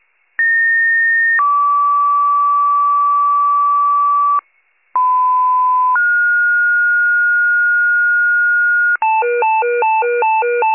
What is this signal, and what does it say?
Two-tone page, heard as a one-second tone, then a three-second tone, about 1850 then 1150 Hz, about 1010 then 1430 Hz, with noise. It is heard from 0.4 to 9.0 s.